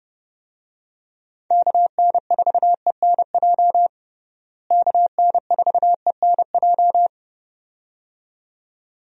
Morse code, heard as a keyed tone, about 30 wpm, 700 Hz. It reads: KN4ENJ KN4ENJ
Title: USA Call Signs - 6 Characters - Repeated 2x - 30wpm